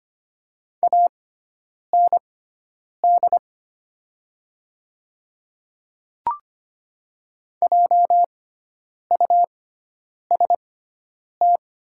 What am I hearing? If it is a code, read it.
Morse code, heard as a keyed tone, about 25 wpm, spaced as for 9 wpm, 690 Hz. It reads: AND JUST
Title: 6x Character Spacing - Top 200 Words - 25wpm